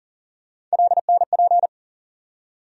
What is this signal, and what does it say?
Morse code, heard as a keyed tone, about 40 wpm, 700 Hz. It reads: LNP
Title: USA Call Sign Postfixes - 3 Letters 40wpm